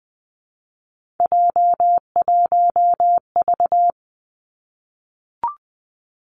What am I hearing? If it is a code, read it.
Morse code, heard as a keyed tone, about 20 wpm, 700 Hz. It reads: J1V
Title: Three Challenging Letter-Number Combinations 20wpm